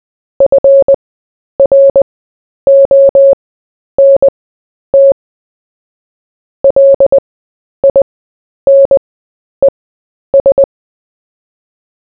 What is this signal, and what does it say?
Morse code, heard as a keyed tone, about 20 wpm, spaced as for 10 wpm, 550 Hz. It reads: FRONT LINES
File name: cw / fists / Short-Story-4-10wpm